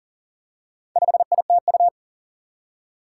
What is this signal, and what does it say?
Morse code, heard as a keyed tone, about 40 wpm, 700 Hz. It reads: 5ITU
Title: Four Challenging Letter-Number Combinations 40wpm